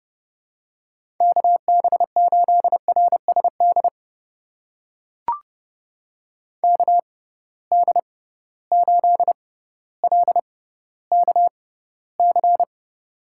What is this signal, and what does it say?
Morse code, heard as a keyed tone, about 30 wpm, spaced as for 15 wpm, 700 Hz. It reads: KB8RSD KD8LKC
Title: USA Call Signs - 6 Characters - 6x Character Spacing - 30wpm